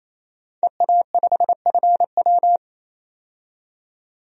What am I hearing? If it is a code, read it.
Morse code, heard as a keyed tone, about 28 wpm, 700 Hz. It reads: EA5FW
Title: DXCC-100 Call Signs - 5 Characters - 28wpm